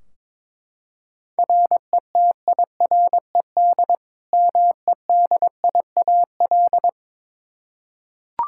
Morse code, heard as a keyed tone, about 22 wpm, 700 Hz. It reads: RETIRED MEDIAL
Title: Practice QSO - Exchange 3 - More Info - 22wpm